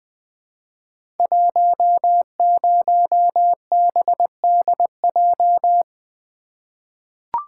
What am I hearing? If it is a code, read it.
Morse code, heard as a keyed tone, about 20 wpm, 700 Hz. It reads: 10BDJ
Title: Code Groups - 5 Letter:Numbers - 20wpm